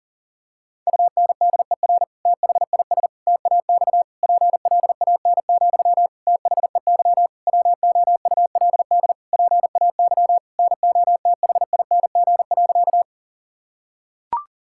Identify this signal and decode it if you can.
Morse code, heard as a keyed tone, about 40 wpm, 700 Hz. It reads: UNDER THIS TAX PLAN, THEY WOULD PAY NOTHING.
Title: Sentences from Top 1000 Words - Review Part 5 - 40wpm